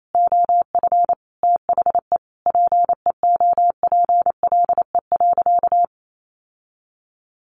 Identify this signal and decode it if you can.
Morse code, heard as a keyed tone, about 28 wpm, 700 Hz. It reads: OF THE PEOPLE.
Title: Sentences from Top 1500 Words 28wpm